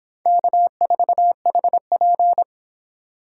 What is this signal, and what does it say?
Morse code, heard as a keyed tone, about 26 wpm, 700 Hz. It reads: K4HP